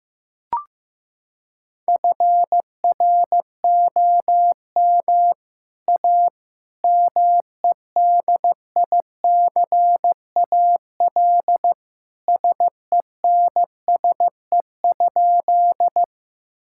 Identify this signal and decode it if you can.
Morse code, heard as a keyed tone, about 15 wpm, 700 Hz. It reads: FROM A MEDICAL SENSE?
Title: Sentences from Top 1000 Words - Review Part 5 - 15wpm